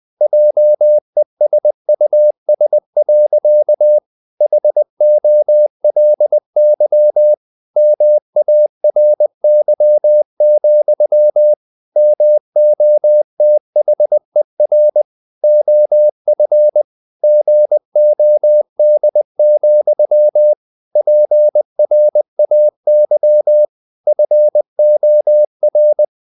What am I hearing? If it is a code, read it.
Morse code, heard as a keyed tone, about 20 wpm, 600 Hz. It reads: JESUS. HOLY MARY, MOTHER OF GOD, PRAY FOR
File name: the old man and the sea / the old man and the sea part 1 20wpm